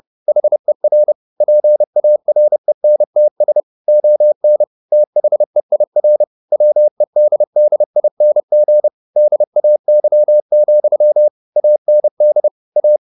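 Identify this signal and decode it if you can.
Morse code, heard as a keyed tone, about 30 wpm, 600 Hz. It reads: HER PARENTS ON THEIR WEDDING DAY, AND A
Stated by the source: Morse code at 30 wpm, the novel Never by Ken Follett in English, from CW med Gnister.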